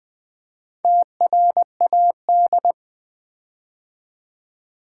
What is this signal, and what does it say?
Morse code, heard as a keyed tone, about 20 wpm, 700 Hz. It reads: TRAD
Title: Build A Word - 4 Letters At A Time - 11-Letter Words - 20wpm